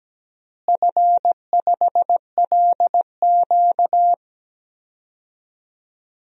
Morse code, heard as a keyed tone, about 17 wpm, 700 Hz. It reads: F5LQ
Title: DXCC-100 Call Signs - 4 Characters - Repeated 3x - 17wpm